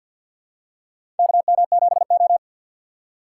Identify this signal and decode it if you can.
Morse code, heard as a keyed tone, about 53 wpm, 700 Hz. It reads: KM7O
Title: USA Call Signs - 4 Characters - 3x Character Spacing - 50wpm